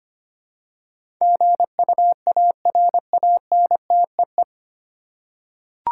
Morse code, heard as a keyed tone, about 25 wpm, 700 Hz. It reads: GUARANTEE
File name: practice / 25wpm / 9 Letter Words - First 3 Letters Given - 25wpm